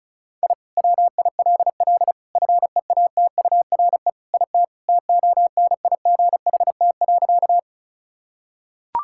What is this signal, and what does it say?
Morse code, heard as a keyed tone, about 34 wpm, 700 Hz. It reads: I WILL FEATURE IT TONIGHT.
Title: Sentences from Top 1000 Words - Review Part 5 - 35wpm